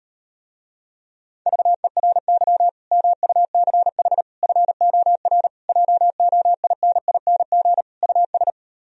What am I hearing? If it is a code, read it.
Morse code, heard as a keyed tone, about 39 wpm, 700 Hz. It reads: VERY MUCH FOR JOINING US